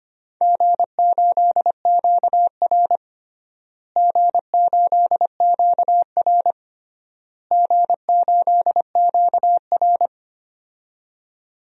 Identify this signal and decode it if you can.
Morse code, heard as a keyed tone, about 25 wpm, 700 Hz. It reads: G8QR G8QR G8QR
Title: DXCC-100 Call Signs - 4 Characters - Repeated 3x - 25wpm